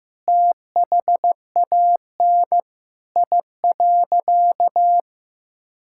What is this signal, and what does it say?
Morse code, heard as a keyed tone, about 15 wpm, 700 Hz. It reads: THAN I.
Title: Sentences from Top 1000 Words - Review Part 5 - 15wpm